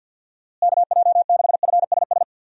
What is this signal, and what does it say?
Morse code, heard as a keyed tone, about 52 wpm, 700 Hz. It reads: KJ6FSS
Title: USA Call Signs - 6 Characters - Repeated 2x - 50wpm